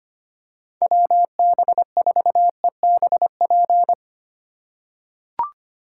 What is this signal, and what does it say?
Morse code, heard as a keyed tone, about 25 wpm, 700 Hz. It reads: WB4EBP
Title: USA Call Signs - 8x Character Spacing - 25wpm